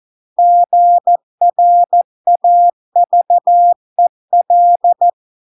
Morse code, heard as a keyed tone, about 14 wpm, 700 Hz. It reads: GRAVEL